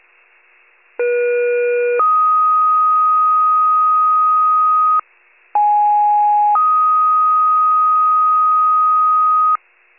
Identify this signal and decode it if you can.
Two-tone page, heard as a one-second tone, then a three-second tone, about 480 then 1250 Hz, about 830 then 1250 Hz. It is heard from 1.0 to 9.6 s.